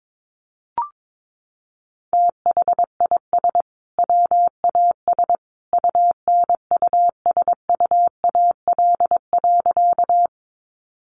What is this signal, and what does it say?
Morse code, heard as a keyed tone, about 22 wpm, 700 Hz. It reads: THIS WAS UNUSUAL.
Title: Sentences from Top 1400 Words 22wpm